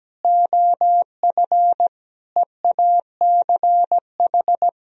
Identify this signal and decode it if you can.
Morse code, heard as a keyed tone, about 17 wpm, 700 Hz. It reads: OF EACH
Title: Sentences from Top 1000 Words - Review Part 3 - 17wpm